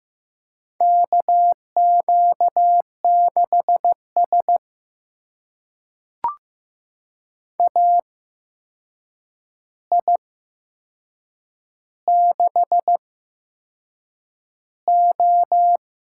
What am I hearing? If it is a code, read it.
Morse code, heard as a keyed tone, about 15 wpm, spaced as for 6 wpm, 700 Hz. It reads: KQ6S AI6O
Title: USA Call Signs - 4 Characters - 8x Character Spacing - 15wpm